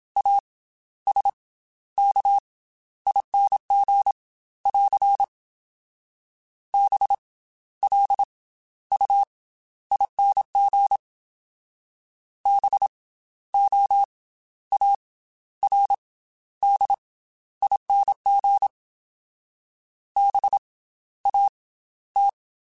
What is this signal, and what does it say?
Morse code, heard as a keyed tone, about 27 wpm, 800 Hz. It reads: A S K ING <AR> B L U ING B O A R D ING B A T
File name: Special INGs / Special INGs-mp3s / Slow ing @25